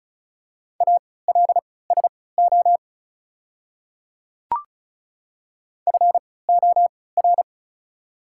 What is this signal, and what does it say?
Morse code, heard as a keyed tone, about 36 wpm, spaced as for 20 wpm, 690 Hz. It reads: ALSO FOR
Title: 3x Character Spacing - Top 300 Words - 35wpm